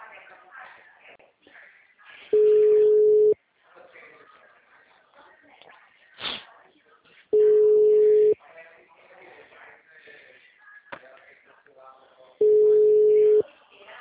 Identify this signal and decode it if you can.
Ringback tone (Europe, 425 Hz), with speech.